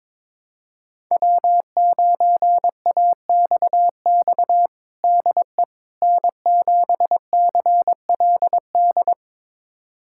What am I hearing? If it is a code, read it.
Morse code, heard as a keyed tone, about 22 wpm, 700 Hz. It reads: W9AXX DE N7CLD
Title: USA Callsign DE Callsign - 5 Characters - 22wpm